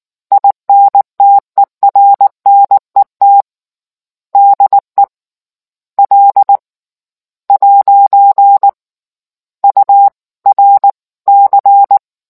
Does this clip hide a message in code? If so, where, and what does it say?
Yes: Morse code from 0.3 to 12.0 s, heard as a keyed tone, about 19 wpm, 800 Hz. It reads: INTERNET DE L ' URC